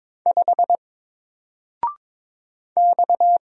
Morse code, heard as a keyed tone, about 22 wpm, 700 Hz. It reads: 5 X
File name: Single-Letter-Number-22wpm